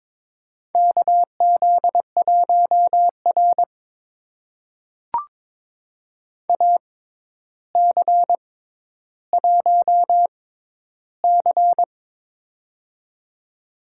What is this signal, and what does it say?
Morse code, heard as a keyed tone, about 22 wpm, spaced as for 11 wpm, 700 Hz. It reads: KZ1R AC1C
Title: USA Call Signs - 4 Characters - 6x Character Spacing - 22wpm